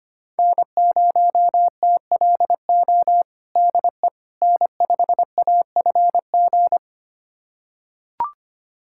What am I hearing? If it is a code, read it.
Morse code, heard as a keyed tone, about 25 wpm, 700 Hz. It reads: N0TLO DE N5AFG
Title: USA Callsign DE Callsign - 5 Characters - 25wpm